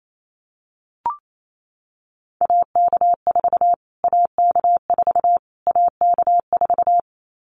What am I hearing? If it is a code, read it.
Morse code, heard as a keyed tone, about 28 wpm, 700 Hz. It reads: AK4 AK4 AK4
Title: Callsign Prefixes for Top 25 DXCC - Repeated 3x - 28wpm